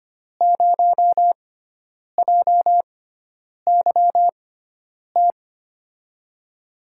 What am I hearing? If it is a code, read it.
Morse code, heard as a keyed tone, about 25 wpm, spaced as for 9 wpm, 700 Hz. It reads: 0JYT
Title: USA Call Signs - 6 Characters - 6x Character Spacing - 25wpm